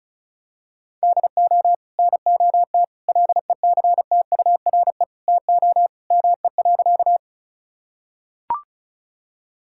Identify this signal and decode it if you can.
Morse code, heard as a keyed tone, about 36 wpm, 700 Hz. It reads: DO NOT LECTURE TO ME.
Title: Sentences from Top 1600 Words 35wpm